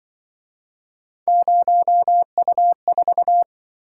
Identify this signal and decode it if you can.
Morse code, heard as a keyed tone, about 25 wpm, 700 Hz. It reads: 0U4
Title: Speed-Racing - Three Letter-Number Combinations - 20wpm